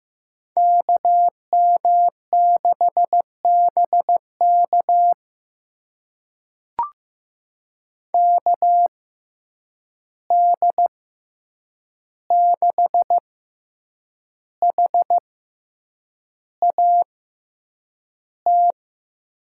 Morse code, heard as a keyed tone, about 15 wpm, spaced as for 7 wpm, 700 Hz. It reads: KM6BK KD6HAT
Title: USA Call Signs - 6x Character Spacing - 15wpm